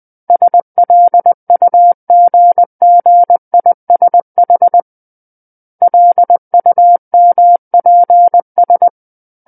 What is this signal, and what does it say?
Morse code, heard as a keyed tone, about 20 wpm, 700 Hz. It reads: SLUGGISH LUMPS
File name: 20 WPM